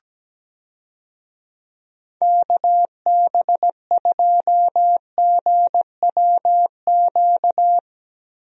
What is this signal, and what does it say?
Morse code, heard as a keyed tone, about 17 wpm, 700 Hz. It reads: KB2GWQ
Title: USA Call Signs - 6 Characters - Repeated 3x - 17wpm